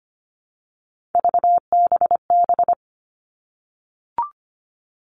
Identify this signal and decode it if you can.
Morse code, heard as a keyed tone, about 25 wpm, 700 Hz. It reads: VBB
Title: Three Challenging Letter-Number Combinations 25wpm